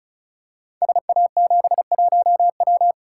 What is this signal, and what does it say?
Morse code, heard as a keyed tone, about 36 wpm, 700 Hz. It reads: SA71W
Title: Code Groups - 5 Letter:Numbers - 35wpm